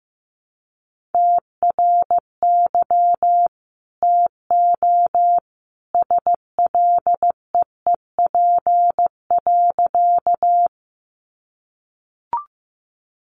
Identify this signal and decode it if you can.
Morse code, heard as a keyed tone, about 15 wpm, 700 Hz. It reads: TRY TO SLEEP.
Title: Sentences from Top 500 Words 15wpm